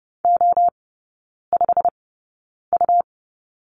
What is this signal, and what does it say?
Morse code, heard as a keyed tone, about 30 wpm, spaced as for 9 wpm, 700 Hz. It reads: O5U
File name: Lesson 17e - Letter u - Three Characters 30wpm